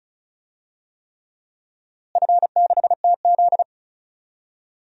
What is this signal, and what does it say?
Morse code, heard as a keyed tone, about 34 wpm, 700 Hz. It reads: F6TZ
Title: DXCC-100 Call Signs - 4 Characters - Repeated 3x - 35wpm